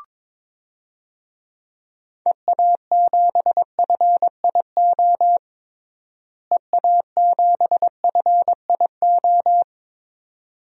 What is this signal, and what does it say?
Morse code, heard as a keyed tone, about 22 wpm, 700 Hz. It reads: EA7FIO EA7FIO
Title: DXCC-100 Call Signs - 6 Characters - Repeated 2x - 22wpm